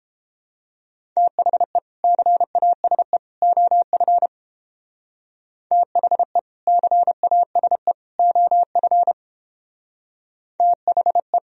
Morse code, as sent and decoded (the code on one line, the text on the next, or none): - .... / -.-. .- ... / --- ..-. / - .... / -.-. .- ... / --- ..-. / - .... .
THE CASE OF THE CASE OF THE